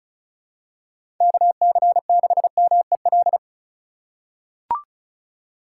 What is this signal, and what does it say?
Morse code, heard as a keyed tone, about 36 wpm, 700 Hz. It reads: KC6MEL
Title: USA Call Signs - 6 Characters - 2x Character Spacing - 35wpm